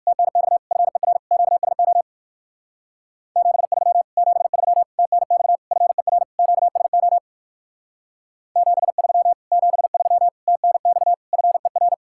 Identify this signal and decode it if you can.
Morse code, heard as a keyed tone, about 59 wpm, 700 Hz. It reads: TNX FER QSO 73 73 TNX FER QSO 73 73 TNX FER